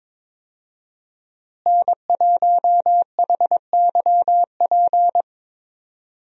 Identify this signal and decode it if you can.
Morse code, heard as a keyed tone, about 22 wpm, 700 Hz. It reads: N1HYP
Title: USA Call Signs - 5 Characters - Repeated 2x - 22wpm